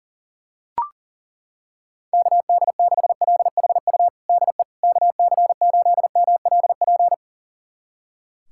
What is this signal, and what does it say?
Morse code, heard as a keyed tone, about 40 wpm, 700 Hz. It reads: KD6LHU DE KC8MLP
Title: USA Callsign DE Callsign 40wpm